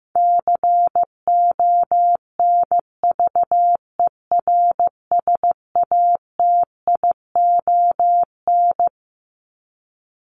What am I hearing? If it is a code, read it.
Morse code, heard as a keyed tone, about 15 wpm, 700 Hz. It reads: CONVERSATION
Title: Top 1751-2000 Words 15wpm